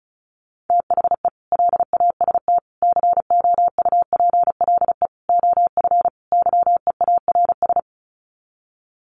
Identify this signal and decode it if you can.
Morse code, heard as a keyed tone, about 34 wpm, 700 Hz. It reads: THE LAST COUPLE OF YEARS
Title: Sets of 5 Words 35wpm